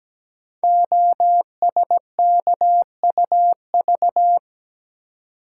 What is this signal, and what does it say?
Morse code, heard as a keyed tone, about 17 wpm, 700 Hz. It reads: OSKUV
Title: Code Groups - 5 Letters - 17wpm